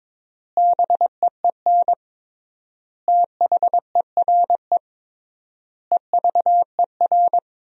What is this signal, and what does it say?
Morse code, heard as a keyed tone, about 22 wpm, 700 Hz. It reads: BEEN THERE EVER
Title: 3x Word-Spacing - Sentences from Top 400 Words - 22wpm